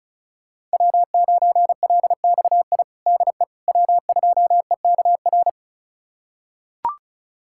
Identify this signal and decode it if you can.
Morse code, heard as a keyed tone, about 34 wpm, 700 Hz. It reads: W9LXI DE W2EKR
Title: USA Callsign DE Callsign - 5 Characters - 35wpm